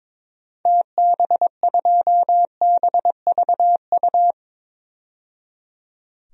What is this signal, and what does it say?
Morse code, heard as a keyed tone, about 22 wpm, 700 Hz. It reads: TB2BVU
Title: DXCC-100 Call Signs - 6 Characters - 22wpm